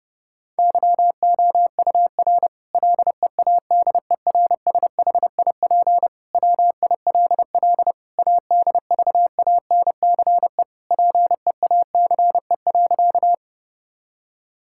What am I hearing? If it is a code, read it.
Morse code, heard as a keyed tone, about 32 wpm, 700 Hz. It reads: YOUR LEADERSHIP WILL ADVANCE PEACE.
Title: Sentences from Top 1600 Words 30wpm